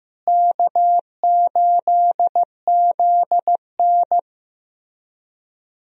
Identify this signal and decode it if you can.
Morse code, heard as a keyed tone, about 15 wpm, 700 Hz. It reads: K8ZN
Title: USA Call Signs - 4 Characters - Repeated 3x - 15wpm